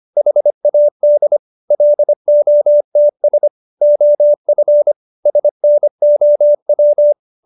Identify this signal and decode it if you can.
Morse code, heard as a keyed tone, about 25 wpm, 590 Hz. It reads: HAD LOTS OF SNOW
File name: QSO312 25